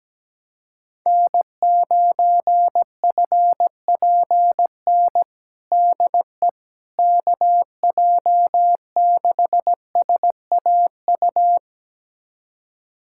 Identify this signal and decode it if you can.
Morse code, heard as a keyed tone, about 17 wpm, 700 Hz. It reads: N9FPN DE KJ6SAU
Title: USA Callsign DE Callsign 17wpm